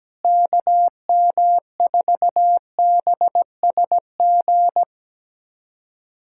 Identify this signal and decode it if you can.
Morse code, heard as a keyed tone, about 17 wpm, 700 Hz. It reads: KM4BSG
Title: USA Callsign DE Callsign - 6 Characters - 17wpm